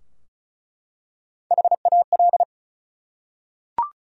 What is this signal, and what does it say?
Morse code, heard as a keyed tone, about 34 wpm, 700 Hz. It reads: HAL
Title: USA Common Names - Max 4 Letters - 35wpm